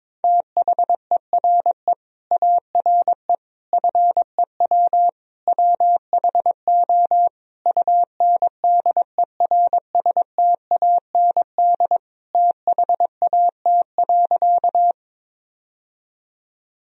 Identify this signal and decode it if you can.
Morse code, heard as a keyed tone, about 22 wpm, 700 Hz. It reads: THERE ARE FEW WHO UNDERSTAND THAT.
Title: Sentences from Top 1000 Words - Review Part 1 - 22wpm